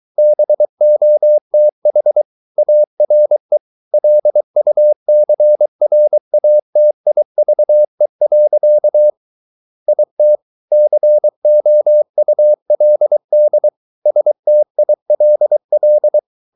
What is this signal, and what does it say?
Morse code, heard as a keyed tone, about 23 wpm, 600 Hz. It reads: BOTH ARE LUCRATIVE. IT COULD STILL